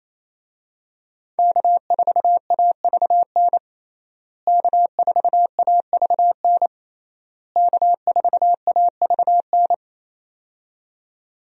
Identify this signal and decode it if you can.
Morse code, heard as a keyed tone, about 28 wpm, 700 Hz. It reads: K4AVN K4AVN K4AVN
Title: USA Call Signs - 5 Characters - Repeated 3x - 28wpm